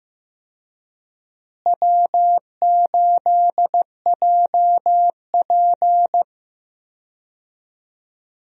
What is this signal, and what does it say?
Morse code, heard as a keyed tone, about 15 wpm, 700 Hz. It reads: W8JP
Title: USA Call Signs - 4 Characters - Repeated 3x - 15wpm